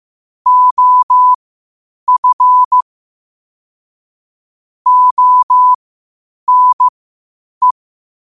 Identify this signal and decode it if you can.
Morse code, heard as a keyed tone, about 15 wpm, spaced as for 8 wpm, 1000 Hz. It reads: OF ONE
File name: qotdfiles / 8wpm